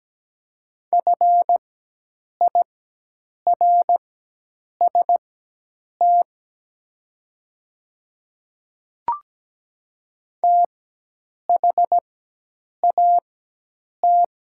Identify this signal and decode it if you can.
Morse code, heard as a keyed tone, about 17 wpm, spaced as for 8 wpm, 700 Hz. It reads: FIRST THAT
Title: 4x Character Spacing - Top 100 Words - 17wpm